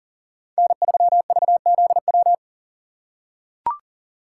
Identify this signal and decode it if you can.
Morse code, heard as a keyed tone, about 40 wpm, 700 Hz. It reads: N3VZW
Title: USA Call Signs - 2x Character Spacing - 40wpm